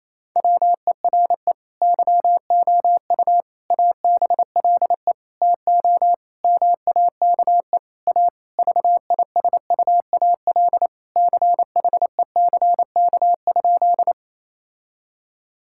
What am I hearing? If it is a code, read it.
Morse code, heard as a keyed tone, about 28 wpm, 700 Hz. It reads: WERE YOU ABLE TO MAKE A VISUAL CHECK?
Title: Sentences from Top 1000 Words - Review Part 6 - 28wpm